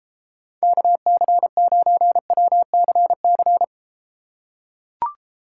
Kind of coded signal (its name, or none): Morse code